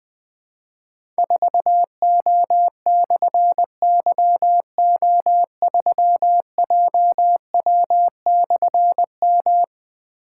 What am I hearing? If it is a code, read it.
Morse code, heard as a keyed tone, about 21 wpm, 700 Hz. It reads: 4O/YO3JW/M
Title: Difficult Call Signs 20wpm